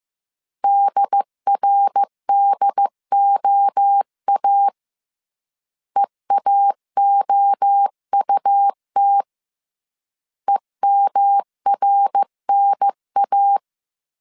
Morse code, heard as a keyed tone, about 15 wpm, 800 Hz. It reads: DRDOA EAOUT EMRNA
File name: CW / DET / 1 3 1500C S A 14wpm 2447